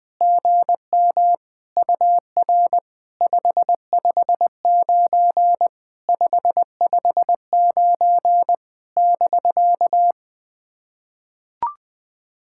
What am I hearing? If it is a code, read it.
Morse code, heard as a keyed tone, about 20 wpm, 700 Hz. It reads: GM UR 559 559 <BK>
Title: POTA QSO Practice - 20wpm